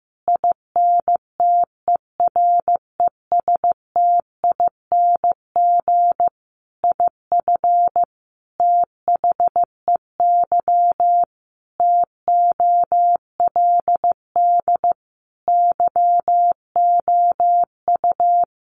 Morse code, heard as a keyed tone, about 15 wpm, 700 Hz. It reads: INTERESTING IF THEY TOLD YOU
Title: Sentences from Top 1000 Words 15wpm